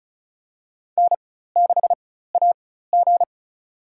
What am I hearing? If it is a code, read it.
Morse code, heard as a keyed tone, about 35 wpm, spaced as for 16 wpm, 700 Hz. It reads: N6AG
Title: USA Call Signs - 4 Characters - 4x Character Spacing - 35wpm